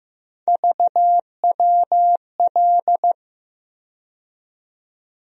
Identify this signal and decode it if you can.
Morse code, heard as a keyed tone, about 15 wpm, 700 Hz. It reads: VWL
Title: USA Call Sign Postfixes - 3 Letters 15wpm